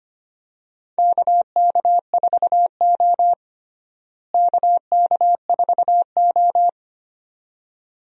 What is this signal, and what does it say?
Morse code, heard as a keyed tone, about 25 wpm, 700 Hz. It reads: KK4O KK4O
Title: USA Call Signs - 4 Characters - Repeated 2x - 25wpm